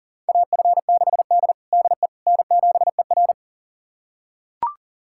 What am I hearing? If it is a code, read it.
Morse code, heard as a keyed tone, about 40 wpm, 700 Hz. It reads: AF6D DE N7ER
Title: USA Callsign DE Callsign - 4 Characters - 40wpm